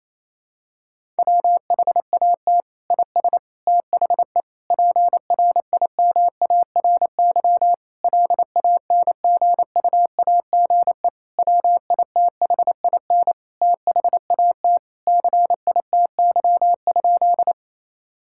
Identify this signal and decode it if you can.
Morse code, heard as a keyed tone, about 28 wpm, 700 Hz. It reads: WHAT IS THE PRIMARY LANGUAGE WITHIN THAT CITY?
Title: Sentences from Top 1000 Words - Review Part 5 - 28wpm